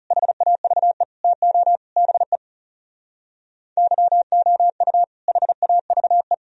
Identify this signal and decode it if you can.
Morse code, heard as a keyed tone, about 39 wpm, 700 Hz. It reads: HAVE TO BE YOU HAVE